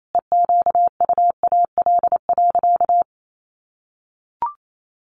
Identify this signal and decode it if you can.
Morse code, heard as a keyed tone, about 28 wpm, 700 Hz. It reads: EQUAL.
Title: Sentences from Top 1500 Words 28wpm